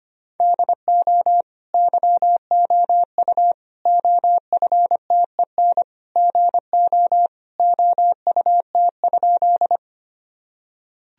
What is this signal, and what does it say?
Morse code, heard as a keyed tone, about 25 wpm, 700 Hz. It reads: DO YOU OFTEN GO OUT?
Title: Sentences from Top 1000 Words - Review Part 1 - 25wpm